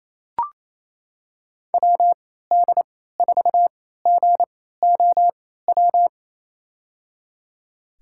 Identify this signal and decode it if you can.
Morse code, heard as a keyed tone, about 28 wpm, spaced as for 16 wpm, 700 Hz. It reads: WD4GOW
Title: USA Call Signs - 3x Character Spacing - 28wpm